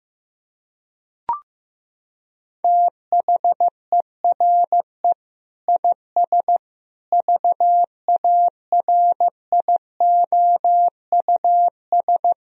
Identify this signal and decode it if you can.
Morse code, heard as a keyed tone, about 15 wpm, 700 Hz. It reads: THERE IS VARIOUS
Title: Sentences from Top 1000 Words - Review Part 4 - 15wpm